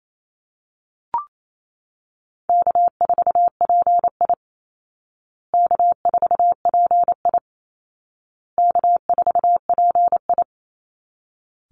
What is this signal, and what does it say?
Morse code, heard as a keyed tone, about 28 wpm, 700 Hz. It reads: K4PI K4PI K4PI
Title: Easy Call Signs Repeated 3x - 28wpm